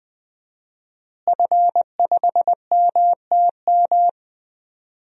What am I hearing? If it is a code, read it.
Morse code, heard as a keyed tone, about 20 wpm, 700 Hz. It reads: F5MTM
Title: DXCC-100 Call Signs - 5 Characters - Repeated 3x - 20wpm